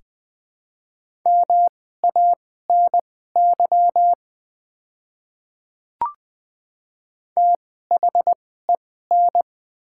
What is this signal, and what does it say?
Morse code, heard as a keyed tone, about 20 wpm, spaced as for 14 wpm, 700 Hz. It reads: MANY THEN
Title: 2x Character Spacing - Top 100 Words - 20wpm